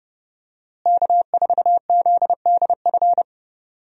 Morse code, heard as a keyed tone, about 30 wpm, 700 Hz. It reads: K4ZDF
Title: USA Call Signs - 5 Characters - Repeated 3x - 30wpm